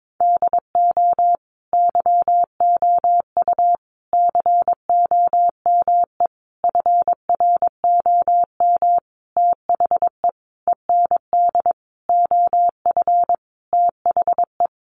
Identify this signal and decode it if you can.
Morse code, heard as a keyed tone, about 22 wpm, 700 Hz. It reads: DO YOU COME FROM THE END OF THE
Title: Sentences from Top 200 Words 22wpm